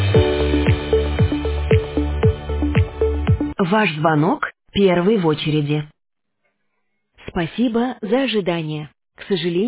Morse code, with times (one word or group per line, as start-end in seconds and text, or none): none